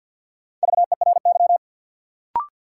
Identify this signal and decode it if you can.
Morse code, heard as a keyed tone, about 51 wpm, 700 Hz. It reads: VERY
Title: Sentences from Top 200 Words - New Word Drill - 50wpm